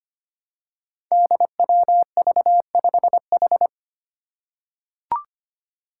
Morse code, heard as a keyed tone, about 25 wpm, 700 Hz. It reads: DWV5H
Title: Five Challenging Letter-Number Combinations 25wpm